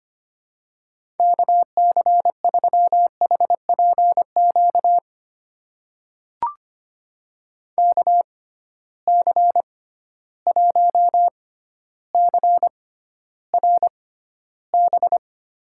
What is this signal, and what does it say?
Morse code, heard as a keyed tone, about 25 wpm, spaced as for 13 wpm, 700 Hz. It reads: KC3HPQ KC1CRB
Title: USA Call Signs - 6 Characters - 6x Character Spacing - 25wpm